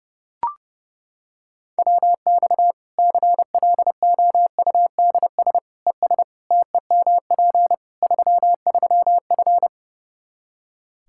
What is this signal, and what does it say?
Morse code, heard as a keyed tone, about 30 wpm, 700 Hz. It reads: WX CLOUDS ES TEMP 33F
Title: K1USN SST QSO Practice 30wpm